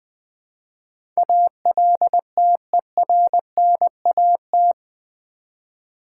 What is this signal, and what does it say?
Morse code, heard as a keyed tone, about 20 wpm, 700 Hz. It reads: ALTERNAT
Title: Build A Word - 4 Letters At A Time - 11-Letter Words - 20wpm